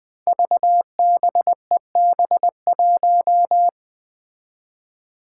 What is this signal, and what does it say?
Morse code, heard as a keyed tone, about 20 wpm, 700 Hz. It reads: VBEB1
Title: Five Challenging Letter-Number Combinations 20wpm